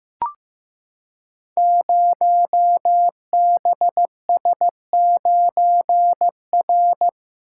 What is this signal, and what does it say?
Morse code, heard as a keyed tone, about 15 wpm, 700 Hz. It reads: 0BS9R
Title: Code Groups - 5 Letter:Numbers - 15wpm